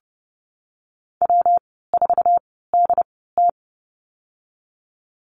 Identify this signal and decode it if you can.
Morse code, heard as a keyed tone, about 30 wpm, spaced as for 17 wpm, 700 Hz. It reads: W4DT
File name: Lesson 19g - Letter w - Call Signs 30wpm